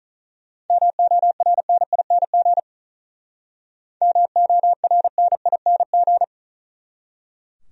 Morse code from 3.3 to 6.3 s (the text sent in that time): MORNING